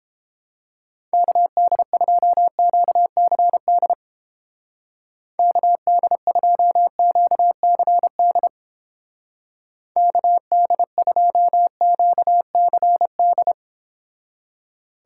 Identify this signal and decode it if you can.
Morse code, heard as a keyed tone, about 29 wpm, 700 Hz. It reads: KD2QCD KD2QCD KD2QCD